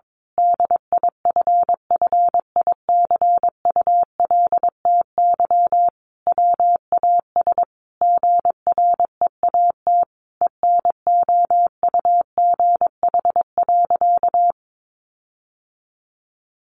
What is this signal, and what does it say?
Morse code, heard as a keyed tone, about 22 wpm, 700 Hz. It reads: DIFFICULTY WAS GREAT ENOUGH.
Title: Sentences from Top 1500 Words 22wpm